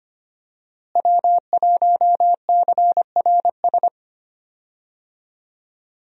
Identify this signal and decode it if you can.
Morse code, heard as a keyed tone, about 25 wpm, 700 Hz. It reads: W1CRS
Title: USA Call Signs - 5 Characters - Repeated 3x - 25wpm